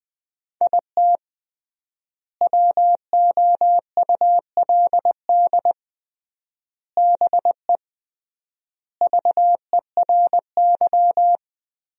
Morse code, heard as a keyed tone, about 20 wpm, 700 Hz. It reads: IT WOULD BE VERY
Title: 3x Word-Spacing - Sentences from Top 200 Words - 20wpm